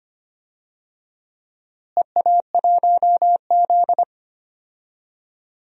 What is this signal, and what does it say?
Morse code, heard as a keyed tone, about 25 wpm, 700 Hz. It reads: EA1Z
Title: DXCC-100 Call Signs - 4 Characters - Repeated 3x - 25wpm